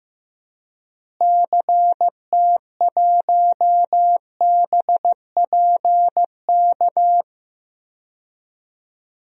Morse code, heard as a keyed tone, about 15 wpm, 700 Hz. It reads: CT1BPK
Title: DXCC-100 Call Signs - 6 Characters - Repeated 3x - 15wpm